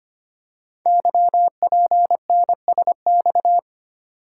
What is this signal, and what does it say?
Morse code, heard as a keyed tone, about 25 wpm, 700 Hz. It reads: YPNSX
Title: Code Groups - 5 Letter:Numbers - 25wpm